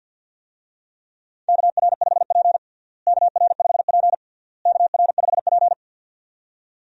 Morse code, heard as a keyed tone, about 52 wpm, 700 Hz. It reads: KR5P KR5P KR5P
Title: USA Call Signs - 4 Characters - Repeated 3x - 50wpm